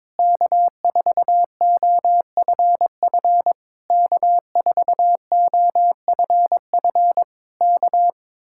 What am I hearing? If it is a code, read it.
Morse code, heard as a keyed tone, about 22 wpm, 700 Hz. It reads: K4OFF K4OFF K
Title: Speed-Racing Common QSO Phrases Encore - 22wpm